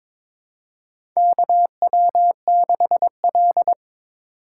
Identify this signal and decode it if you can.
Morse code, heard as a keyed tone, about 21 wpm, 700 Hz. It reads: KW6L
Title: USA Call Signs - 2x Character Spacing - 22wpm